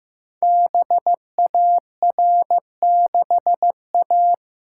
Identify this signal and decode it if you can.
Morse code, heard as a keyed tone, about 15 wpm, 700 Hz. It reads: BAR6A
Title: Code Groups - 5 Letter:Numbers - 15wpm